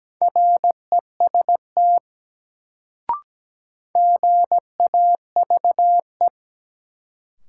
Morse code, heard as a keyed tone, about 17 wpm, 700 Hz. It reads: REST GAVE
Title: Sentences from Top 500 Words - New Word Drill - 17wpm